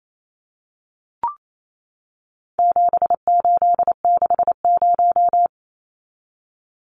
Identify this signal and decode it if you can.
Morse code, heard as a keyed tone, about 28 wpm, 700 Hz. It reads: 7860
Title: Numbers - 4-Digit 28wpm